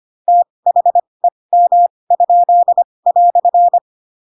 Morse code, heard as a keyed tone, about 25 wpm, spaced as for 20 wpm, 700 Hz. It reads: THEM?"
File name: Pinocchio / Pinocchio ch7 15 25 ik1hge